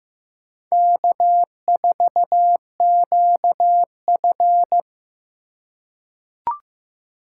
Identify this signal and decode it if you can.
Morse code, heard as a keyed tone, about 15 wpm, 700 Hz. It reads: K4QF
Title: USA Call Signs - 4x Character Spacing - 15wpm